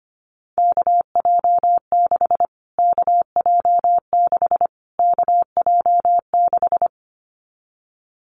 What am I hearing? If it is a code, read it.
Morse code, heard as a keyed tone, about 25 wpm, 700 Hz. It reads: KJ6 KJ6 KJ6
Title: Callsign Prefixes for Top 25 DXCC - Repeated 3x - 25wpm